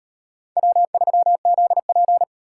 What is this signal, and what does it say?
Morse code, heard as a keyed tone, about 36 wpm, 700 Hz. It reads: W3ZP